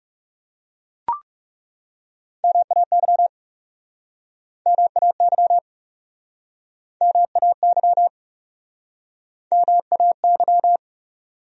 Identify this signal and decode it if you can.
Morse code, heard as a keyed tone, about 39 wpm, 700 Hz. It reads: MAY MAY MAY MAY